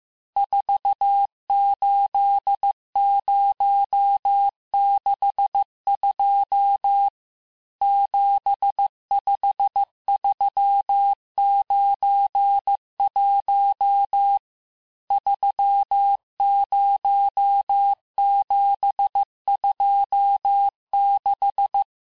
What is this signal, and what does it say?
Morse code, heard as a keyed tone, about 15 wpm, 800 Hz. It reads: 48062 75391 30726
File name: cours-du-lundi files / cours-1585-texte-chiffre-vitesse-14mpm